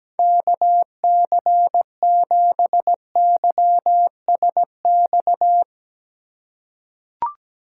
Morse code, heard as a keyed tone, about 17 wpm, 700 Hz. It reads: KC7YSX
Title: USA Call Signs - 2x Character Spacing - 17wpm